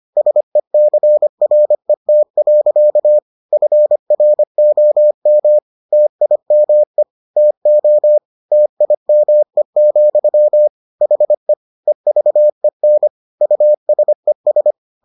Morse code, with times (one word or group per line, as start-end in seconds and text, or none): 0.2-3.2 SECRET.
3.5-5.6 FROM
5.9-7.0 TIME
7.4-8.2 TO
8.5-10.7 TIME,
11.0-11.5 HE
11.9-13.1 EVEN
13.4-14.7 USES